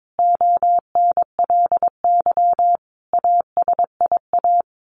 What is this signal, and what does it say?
Morse code, heard as a keyed tone, about 22 wpm, 700 Hz. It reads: ONLY ASIA
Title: A Sound Of Thunder 22wpm